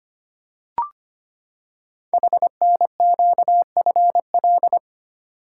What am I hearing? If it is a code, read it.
Morse code, heard as a keyed tone, about 25 wpm, 700 Hz. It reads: HNQFL
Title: Code Groups - 5 Letters - 25wpm